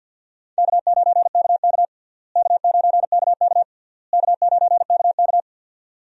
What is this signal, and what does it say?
Morse code, heard as a keyed tone, about 48 wpm, 700 Hz. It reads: K9KK K9KK K9KK
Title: USA Call Signs - 4 Characters - Repeated 3x - 50wpm